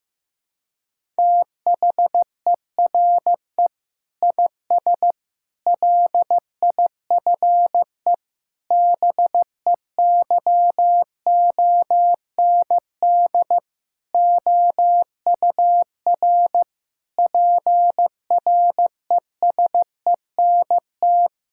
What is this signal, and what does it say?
Morse code, heard as a keyed tone, about 15 wpm, 700 Hz. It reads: THERE IS LIFE BEYOND OUR PRESENT
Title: Sentences from Top 1000 Words - Review Part 6 - 15wpm